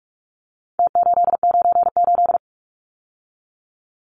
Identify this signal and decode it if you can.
Morse code, heard as a keyed tone, about 44 wpm, 700 Hz. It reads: T898
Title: Numbers - Contest Style - 4-Digit 45wpm